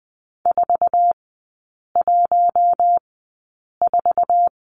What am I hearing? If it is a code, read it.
Morse code, heard as a keyed tone, about 20 wpm, spaced as for 8 wpm, 700 Hz. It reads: 414
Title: 067 Lesson 12e - Letter d - Three Characters 20wpm